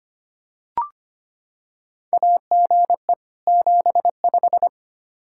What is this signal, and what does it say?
Morse code, heard as a keyed tone, about 25 wpm, 700 Hz. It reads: AGE 75
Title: Practice QSO - Exchange 3 - More Info - 25wpm